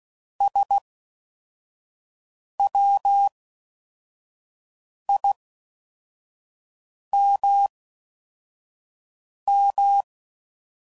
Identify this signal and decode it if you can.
Morse code, heard as a keyed tone, about 16 wpm, spaced as for 4 wpm, 800 Hz. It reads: SWIMM